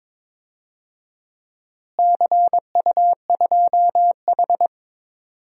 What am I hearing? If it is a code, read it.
Morse code, heard as a keyed tone, about 22 wpm, 700 Hz. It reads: CU2H
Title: DXCC-100 Call Signs - 4 Characters - Repeated 3x - 22wpm